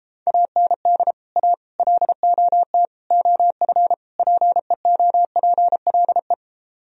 Morse code, heard as a keyed tone, about 29 wpm, 700 Hz. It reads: AND A LOT OF PEOPLE